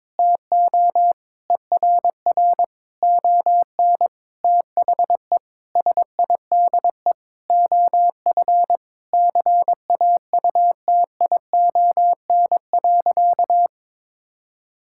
Morse code, heard as a keyed tone, about 22 wpm, 700 Hz. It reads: TO ERR ON THE SIDE OF CAUTION.
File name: Sentences with 7-Word Phrases - 22wpm